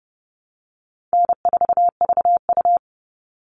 Morse code, heard as a keyed tone, about 30 wpm, 700 Hz. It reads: N4VU
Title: All Cut Numbers - 4-Digit 30wpm